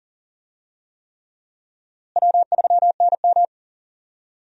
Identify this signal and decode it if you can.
Morse code, heard as a keyed tone, about 37 wpm, 700 Hz. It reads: W3NM
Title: USA Call Signs - 4 Characters - Repeated 3x - 40wpm